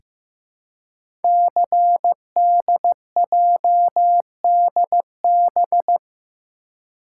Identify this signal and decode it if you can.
Morse code, heard as a keyed tone, about 15 wpm, 700 Hz. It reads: CDJDB
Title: Code Groups - 5 Letters - 15wpm